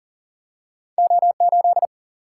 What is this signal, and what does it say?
Morse code, heard as a keyed tone, about 39 wpm, 700 Hz. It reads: O8
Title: Single-Letter Single-Number 40wpm